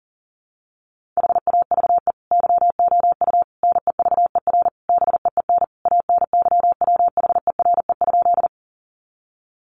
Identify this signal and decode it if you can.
Morse code, heard as a keyed tone, about 40 wpm, 700 Hz. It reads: HAVE YOU NEVER BEEN ANYWHERE?
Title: Sentences from Top 1400 Words 40wpm